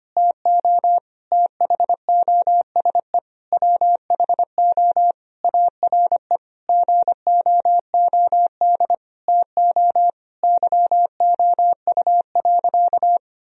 Morse code, heard as a keyed tone, about 25 wpm, 700 Hz. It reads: TO THOSE WHO ARE GOOD TO YOU.
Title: Sentences from Top 1000 Words - Review Part 1 - 25wpm